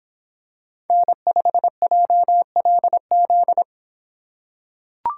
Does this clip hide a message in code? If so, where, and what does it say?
Yes: Morse code from 0.9 to 3.6 s, heard as a keyed tone, about 26 wpm, 700 Hz. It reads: N5JLZ